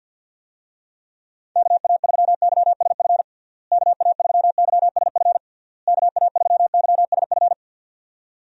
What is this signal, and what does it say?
Morse code, heard as a keyed tone, about 52 wpm, 700 Hz. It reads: KA3YSF KA3YSF KA3YSF
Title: USA Call Signs - 6 Characters - Repeated 3x - 50wpm